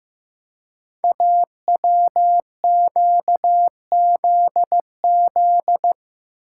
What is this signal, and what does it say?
Morse code, heard as a keyed tone, about 15 wpm, 700 Hz. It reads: AWQZZ
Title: Code Groups - 5 Letters - 15wpm